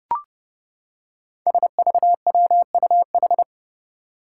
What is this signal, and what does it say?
Morse code, heard as a keyed tone, about 30 wpm, 700 Hz. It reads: SVWUH
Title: Five Challenging Letter-Number Combinations 30wpm